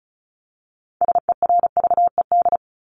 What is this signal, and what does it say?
Morse code, heard as a keyed tone, about 34 wpm, 700 Hz. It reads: SERVED
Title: ED Suffix - Top 100 Words with Base Words 35wpm